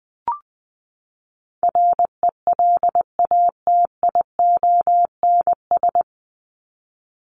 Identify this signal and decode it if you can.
Morse code, heard as a keyed tone, about 19 wpm, 700 Hz. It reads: RELATIONS
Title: ES:S Suffix - Top 100 Words with Base Words 20wpm